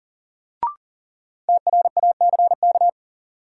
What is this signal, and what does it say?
Morse code, heard as a keyed tone, about 40 wpm, 700 Hz. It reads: TRACK
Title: Sentences from Top 1100 Words - New Word Drill - 40wpm